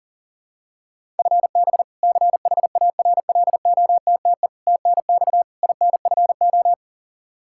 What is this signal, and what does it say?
Morse code, heard as a keyed tone, about 40 wpm, 700 Hz. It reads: FB CHARLOTTE TNX INFO
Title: Practice QSO - Exchange 3 - More Info - 40wpm